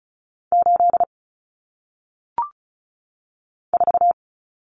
Morse code, heard as a keyed tone, about 36 wpm, 700 Hz. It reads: 8 4
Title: Numbers - 1-Digit 35wpm